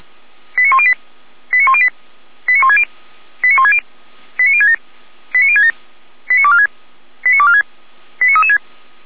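Five-tone selective call (ZVEI), heard as quick runs of five tones, tones of about 70 ms. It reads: 89108 89108 89160 89160 89067 89067 89246 89246 89336
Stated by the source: ZVEI five-tone selective call received on an SDR.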